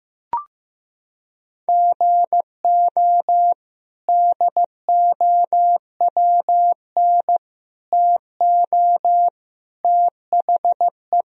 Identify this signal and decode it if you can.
Morse code, heard as a keyed tone, about 15 wpm, 700 Hz. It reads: GO DOWN TO THE
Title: Sentences from Top 1000 Words - Review Part 4 - 15wpm